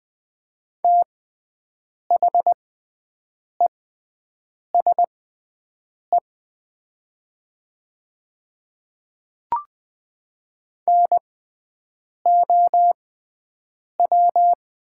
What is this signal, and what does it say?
Morse code, heard as a keyed tone, about 20 wpm, spaced as for 7 wpm, 700 Hz. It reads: THESE NOW
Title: 6x Character Spacing - Top 100 Words - 20wpm